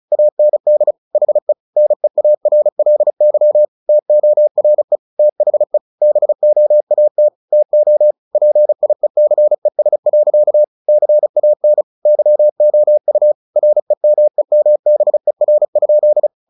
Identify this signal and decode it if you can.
Morse code, heard as a keyed tone, about 34 wpm, 600 Hz. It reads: AND HE NEARLY TORE THE BOAT TO PIECES. CAN YOU REMEMBER?
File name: the old man and the sea / the old man and the sea 1 35wpm